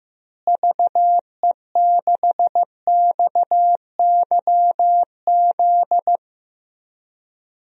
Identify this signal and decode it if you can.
Morse code, heard as a keyed tone, about 15 wpm, 700 Hz. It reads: VE6XYZ
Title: DXCC-100 Call Signs - 6 Characters - Repeated 3x - 15wpm